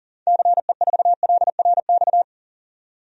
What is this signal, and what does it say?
Morse code, heard as a keyed tone, about 40 wpm, 700 Hz. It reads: CE4LRX
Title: DXCC-100 Call Signs - 6 Characters - 40wpm